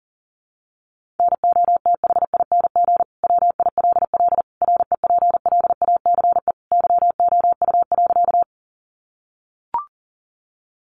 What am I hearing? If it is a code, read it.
Morse code, heard as a keyed tone, about 40 wpm, 700 Hz. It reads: NOTHING WILL REPLACE YOU.
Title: Sentences from Top 1400 Words 40wpm